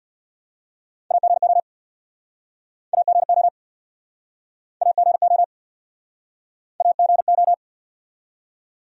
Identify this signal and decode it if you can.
Morse code, heard as a keyed tone, about 67 wpm, 690 Hz. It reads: AGO AGO AGO AGO